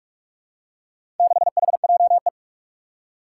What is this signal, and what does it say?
Morse code, heard as a keyed tone, about 44 wpm, 700 Hz. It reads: 6HJE
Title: Four Challenging Letter-Number Combinations 45wpm